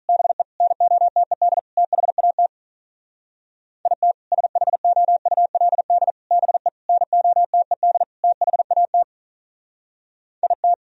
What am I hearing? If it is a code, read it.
Morse code, heard as a keyed tone, about 45 wpm, 700 Hz. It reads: BE NOTED THAT IT SHOULD BE NOTED THAT IT